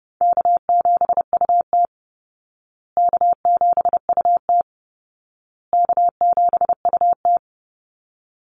Morse code, heard as a keyed tone, about 30 wpm, 700 Hz. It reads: K7UT K7UT K7UT
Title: Easy Call Signs Repeated 3x - 30wpm